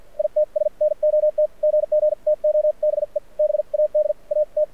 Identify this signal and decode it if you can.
Morse code, heard as a keyed tone, about 53 wpm, spaced as for 44 wpm, 600 Hz. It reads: ITS NOT GG TO BE BAD AT
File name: archived / CRSnet